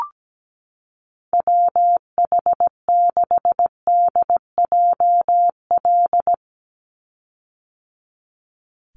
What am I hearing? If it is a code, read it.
Morse code, heard as a keyed tone, about 17 wpm, 700 Hz. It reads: WH6DJL
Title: USA Call Signs 17wpm